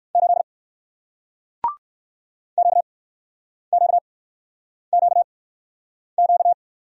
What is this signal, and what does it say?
Morse code, heard as a keyed tone, about 60 wpm, spaced as for 9 wpm, 700 Hz. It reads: 8 QQQQ